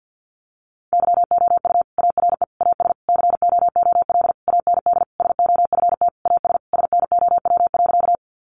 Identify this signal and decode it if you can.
Morse code, heard as a keyed tone, about 50 wpm, 700 Hz. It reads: YOU ARE AS COOL AND SOFT AS SNOW.
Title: Sentences from Top 1500 Words 50wpm